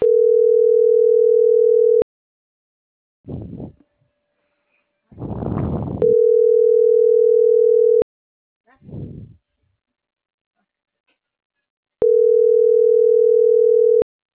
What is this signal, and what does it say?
Ringback tone (North America), 440 + 480 Hz.